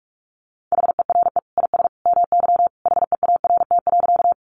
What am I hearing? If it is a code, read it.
Morse code, heard as a keyed tone, about 44 wpm, 700 Hz. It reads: HERE IS MY HEART.
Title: Sentences from Top 1200 Words 45wpm